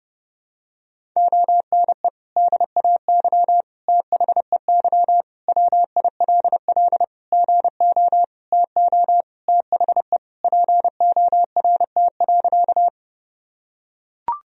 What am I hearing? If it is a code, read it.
Morse code, heard as a keyed tone, about 30 wpm, 700 Hz. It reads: ONE DAY THEY WILL GO TO THE PORT.
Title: Sentences from Top 1000 Words - Review Part 1 - 30wpm